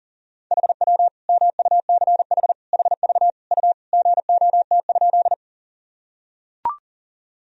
Morse code, heard as a keyed tone, about 40 wpm, 700 Hz. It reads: HW MUCH HV U GOT?
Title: Sentences from Top 300 Words and Abbreviations 40wpm